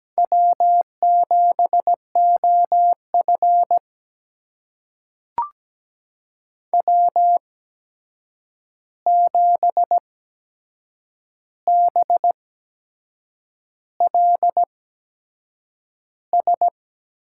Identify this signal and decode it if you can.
Morse code, heard as a keyed tone, about 17 wpm, spaced as for 7 wpm, 700 Hz. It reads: W7OF W7BLS
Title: USA Call Signs - 8x Character Spacing - 17wpm